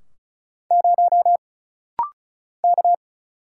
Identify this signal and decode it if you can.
Morse code, heard as a keyed tone, about 36 wpm, 700 Hz. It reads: 0 K
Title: Single Letter-Number Alt - Rapid-Fire 35wpm